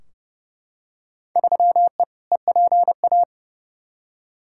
Morse code, heard as a keyed tone, about 30 wpm, 700 Hz. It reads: 3E EPA
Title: Field Day Exchanges Encore - 30wpm